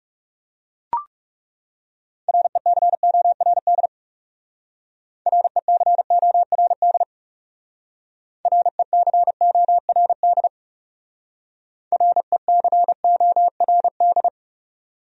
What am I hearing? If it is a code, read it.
Morse code, heard as a keyed tone, about 37 wpm, 700 Hz. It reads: RECORD RECORD RECORD RECORD
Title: Speed-Racing Top 500 Words 30wpm